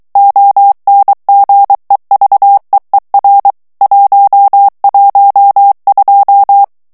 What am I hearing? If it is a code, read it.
Morse code, heard as a keyed tone, about 23 wpm, 800 Hz. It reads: ONGEVEER 112
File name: HOLLANDAIS texte clair 1 vitesse 22 mots minute